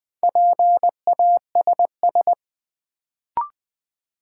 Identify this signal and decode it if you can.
Morse code, heard as a keyed tone, about 20 wpm, 700 Hz. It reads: PASS
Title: Sentences from Top 400 Words - New Word Drill - 20wpm